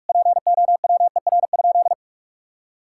Morse code, heard as a keyed tone, about 41 wpm, 700 Hz. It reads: POWER?
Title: Sentences from Top 1000 Words - Review Part 5 - 45wpm